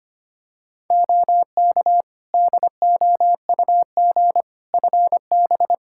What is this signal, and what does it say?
Morse code, heard as a keyed tone, about 25 wpm, 700 Hz. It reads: OK DOUG FB
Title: Practice QSO - Exchange 2 - Info - 25wpm